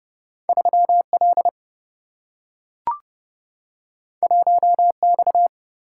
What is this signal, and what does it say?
Morse code, heard as a keyed tone, about 30 wpm, 700 Hz. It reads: 3L 1X
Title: Single-Letter Single-Number 30wpm